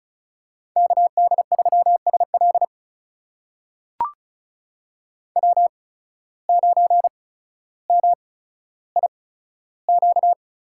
Morse code, heard as a keyed tone, about 36 wpm, spaced as for 15 wpm, 700 Hz. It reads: KD3SL W9MIQ
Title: USA Call Signs - 5 Characters - 8x Character Spacing - 35wpm